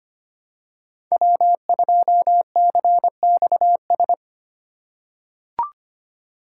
Morse code, heard as a keyed tone, about 25 wpm, 700 Hz. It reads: W2CXS
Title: Code Groups - 5 Letter:Numbers - 25wpm